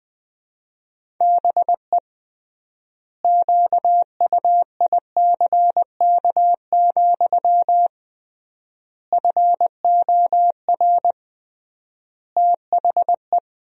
Morse code, heard as a keyed tone, about 20 wpm, 700 Hz. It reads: BE QUICK, FOR THE
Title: 3x Word-Spacing - Sentences from Top 500 Words - 20wpm